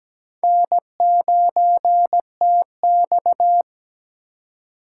Morse code, heard as a keyed tone, about 17 wpm, 700 Hz. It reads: N9TX
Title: USA Call Signs - 4 Characters - 2x Character Spacing - 17wpm